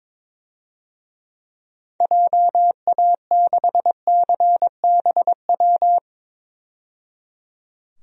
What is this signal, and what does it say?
Morse code, heard as a keyed tone, about 23 wpm, 700 Hz. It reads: JA6CBW
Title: DXCC-100 Call Signs - 6 Characters - 22wpm